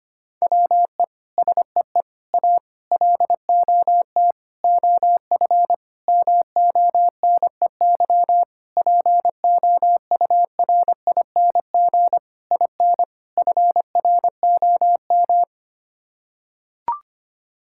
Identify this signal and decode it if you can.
Morse code, heard as a keyed tone, about 25 wpm, 700 Hz. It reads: WE SEE A LOT OF MONEY POURING IN FROM
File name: Sets of 9 Words - 25wpm